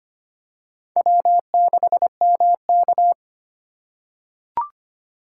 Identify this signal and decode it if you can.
Morse code, heard as a keyed tone, about 25 wpm, 700 Hz. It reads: W6MK
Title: USA Call Signs - 4 Characters - 4x Character Spacing - 25wpm